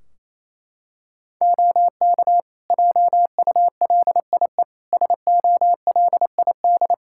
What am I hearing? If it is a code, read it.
Morse code, heard as a keyed tone, about 28 wpm, 700 Hz. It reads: OK JULIE SOLID
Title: Practice QSO - Exchange 3 - More Info - 28wpm